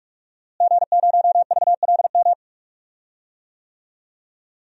Morse code, heard as a keyed tone, about 44 wpm, 700 Hz. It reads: G0VLM
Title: DXCC-100 Call Signs - 5 Characters - Repeated 2x - 45wpm